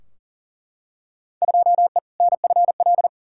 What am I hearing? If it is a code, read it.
Morse code, heard as a keyed tone, about 40 wpm, 700 Hz. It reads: 2E NFL
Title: Field Day Exchanges Encore - 40wpm